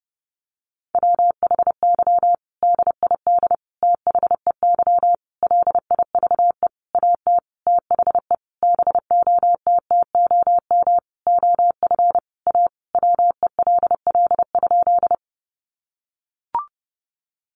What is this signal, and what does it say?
Morse code, heard as a keyed tone, about 30 wpm, 700 Hz. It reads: WHY DID THEY LIVE AT THE BOTTOM OF A WELL?
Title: Sentences from Top 900 Words 30wpm